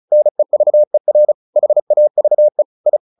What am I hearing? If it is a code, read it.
Morse code, heard as a keyed tone, about 34 wpm, 600 Hz. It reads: NEVER HAVE I
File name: the old man and the sea 2 35wpm